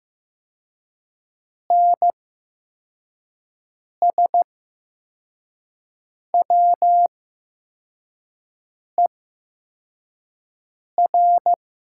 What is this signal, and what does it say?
Morse code, heard as a keyed tone, about 15 wpm, spaced as for 4 wpm, 700 Hz. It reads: NSWER
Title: 8x Character Spacing - Top 300 Words - 15wpm